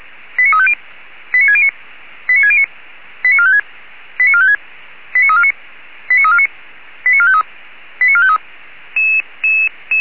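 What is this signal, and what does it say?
Five-tone selective call (ZVEI), heard as quick runs of five tones, tones of about 70 ms. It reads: 89350 89609 89609 89456 89456 89349 89349 89453 89453
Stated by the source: ZVEI five-tone selective call received by SDR, fire brigade Orlamünde (feuerwehr-orlamuende) server.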